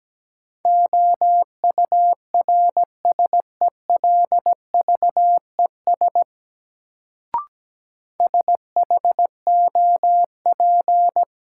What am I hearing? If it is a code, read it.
Morse code, heard as a keyed tone, about 17 wpm, 700 Hz. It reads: OURSELVES SHOP
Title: Sentences from Top 1600 Words - New Word Drill Alt - 17wpm